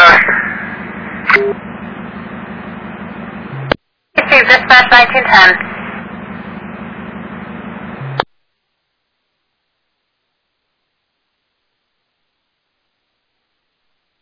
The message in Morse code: none

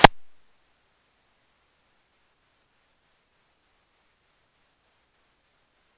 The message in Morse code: none

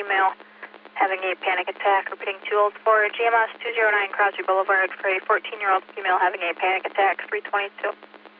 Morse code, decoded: none

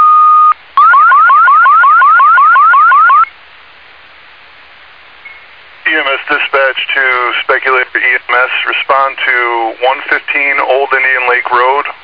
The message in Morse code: none

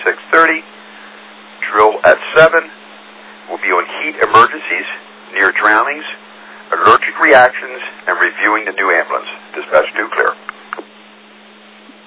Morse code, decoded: none

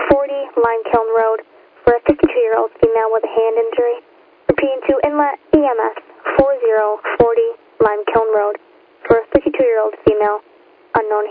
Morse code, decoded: none

